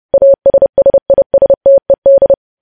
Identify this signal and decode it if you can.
Morse code, heard as a keyed tone, about 30 wpm, 550 Hz. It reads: ASSISTED
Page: News Headlines in Morse Code at 30 WPM